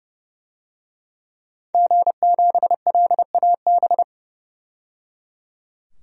Morse code, heard as a keyed tone, about 30 wpm, 700 Hz. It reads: G7LAB
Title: DXCC-100 Call Signs - 5 Characters - 30wpm